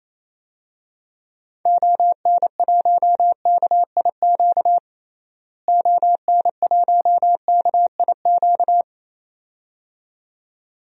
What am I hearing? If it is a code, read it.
Morse code, heard as a keyed tone, about 28 wpm, 700 Hz. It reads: ON1KIQ ON1KIQ
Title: DXCC-100 Call Signs - 6 Characters - Repeated 2x - 28wpm